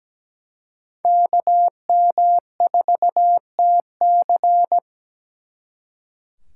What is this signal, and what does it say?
Morse code, heard as a keyed tone, about 20 wpm, 700 Hz. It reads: KM4TC
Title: Speed Racing - USA 2x2 Call Signs 17wpm